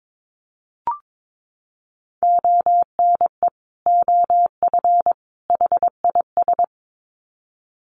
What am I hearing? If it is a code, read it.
Morse code, heard as a keyed tone, about 22 wpm, 700 Hz. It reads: ONE OF HIS
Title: Sets of 3 Words 22wpm